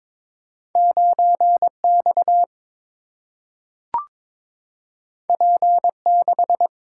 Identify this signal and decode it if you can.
Morse code, heard as a keyed tone, about 23 wpm, 700 Hz. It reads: 9X P6
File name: practice / 22wpm / Two Letter-Number Combinations - 22wpm